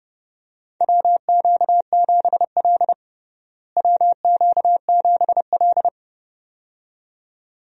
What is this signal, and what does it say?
Morse code, heard as a keyed tone, about 32 wpm, 700 Hz. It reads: WQ7L WQ7L
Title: USA Call Signs - 4 Characters - Repeated 2x - 30wpm